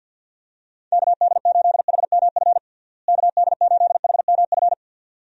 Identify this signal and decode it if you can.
Morse code, heard as a keyed tone, about 52 wpm, 700 Hz. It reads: KD8HMF KD8HMF
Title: USA Call Signs - 6 Characters - Repeated 2x - 50wpm